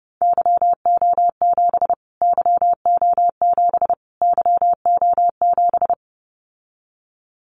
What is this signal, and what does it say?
Morse code, heard as a keyed tone, about 30 wpm, 700 Hz. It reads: YO7 YO7 YO7
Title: Callsign Prefixes for Top 25 DXCC - Repeated 3x - 30wpm